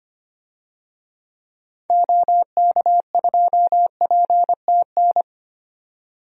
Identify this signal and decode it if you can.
Morse code, heard as a keyed tone, about 25 wpm, 700 Hz. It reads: OK2PTN
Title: DXCC-100 Call Signs - 6 Characters - 25wpm